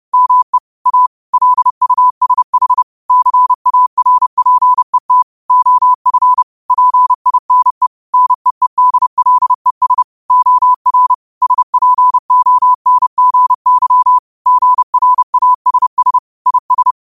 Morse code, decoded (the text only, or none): ME A LUSH CARPET OF PINE NEEDLES OR SPONGY GRASS IS